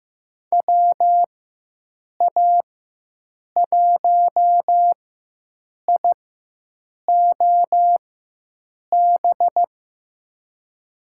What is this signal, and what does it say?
Morse code, heard as a keyed tone, about 15 wpm, spaced as for 7 wpm, 700 Hz. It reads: WA1IOB
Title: USA Call Signs - 6 Characters - 4x Character Spacing - 15wpm